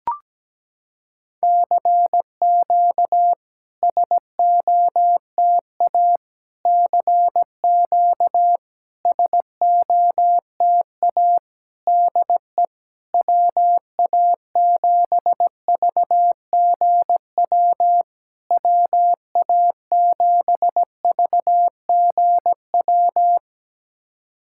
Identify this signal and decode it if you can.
Morse code, heard as a keyed tone, about 17 wpm, 700 Hz. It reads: CQ SOTA CQ SOTA DE WA7VGW WA7VGW
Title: SOTA QSO Practice 17wpm